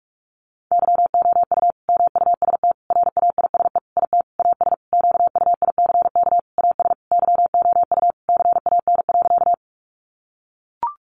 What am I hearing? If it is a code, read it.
Morse code, heard as a keyed tone, about 44 wpm, 700 Hz. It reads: YOU MUST RAISE IT AS QUICK AS YOU CAN.
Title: Sentences from Top 1400 Words 45wpm